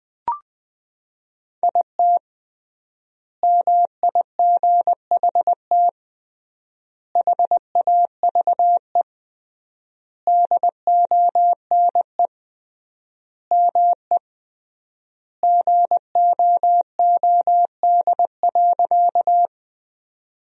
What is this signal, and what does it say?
Morse code, heard as a keyed tone, about 20 wpm, 700 Hz. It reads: IT MIGHT HAVE DONE ME GOOD.
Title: 3x Word-Spacing - Sentences from Top 500 Words - 20wpm